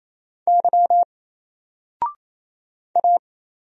Morse code, heard as a keyed tone, about 27 wpm, 700 Hz. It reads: Y A
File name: Single-Letter-Number-28wpm